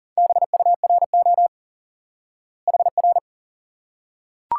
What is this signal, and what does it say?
Morse code, heard as a keyed tone, about 40 wpm, 700 Hz. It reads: BURO HR
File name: IQE Course 3 - Lesson 89 - YAGI - Two Elements - 40wpm